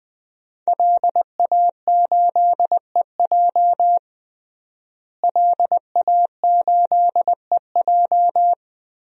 Morse code, heard as a keyed tone, about 20 wpm, 700 Hz. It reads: LA8EJ LA8EJ
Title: DXCC-100 Call Signs - 5 Characters - Repeated 2x - 20wpm